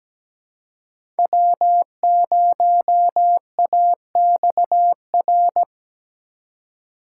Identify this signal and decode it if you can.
Morse code, heard as a keyed tone, about 17 wpm, 700 Hz. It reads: W0AXR
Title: USA Call Signs - 5 Characters - 3x Character Spacing - 17wpm